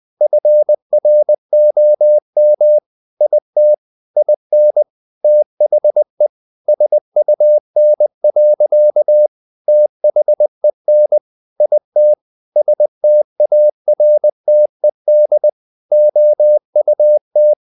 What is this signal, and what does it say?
Morse code, heard as a keyed tone, about 20 wpm, 600 Hz. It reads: FROM IT IN THE SUN. THEN IT STARTED OUT
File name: the old man and the sea part 2 20wpm